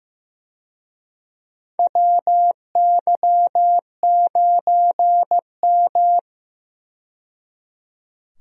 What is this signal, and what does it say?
Morse code, heard as a keyed tone, about 15 wpm, 700 Hz. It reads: WY9M
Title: USA Call Signs - 4 Characters - 15wpm